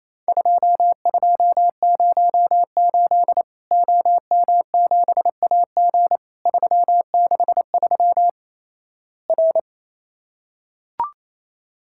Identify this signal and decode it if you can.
Morse code, heard as a keyed tone, about 27 wpm, 700 Hz. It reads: 2208 OM7AG 363 R
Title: Worked All Europe DX Contest - QTC Exchange - 28wpm